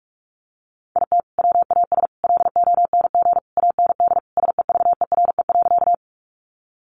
Morse code, heard as a keyed tone, about 44 wpm, 700 Hz. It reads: IT WAS LONG AND SEVERE.
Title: Sentences from Top 1400 Words 45wpm